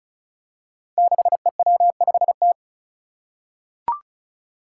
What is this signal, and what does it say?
Morse code, heard as a keyed tone, about 36 wpm, 700 Hz. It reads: 6EW5T
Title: Five Challenging Letter-Number Combinations 35wpm